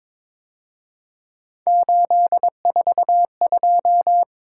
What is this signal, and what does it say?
Morse code, heard as a keyed tone, about 26 wpm, 700 Hz. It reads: 842
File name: Speed-Racing - Three Letter-Number Combinations - 22wpm